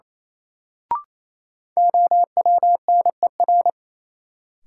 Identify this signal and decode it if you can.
Morse code, heard as a keyed tone, about 28 wpm, 700 Hz. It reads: OWNER
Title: Sentences from Top 1600 Words - New Word Drill - 28wpm